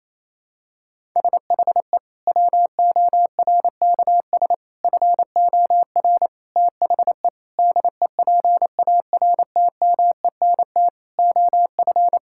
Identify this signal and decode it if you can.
Morse code, heard as a keyed tone, about 28 wpm, 700 Hz. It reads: SHE WORKS FOR THE DEPARTMENT OF
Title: Sentences from Top 1000 Words - Review Part 5 - 28wpm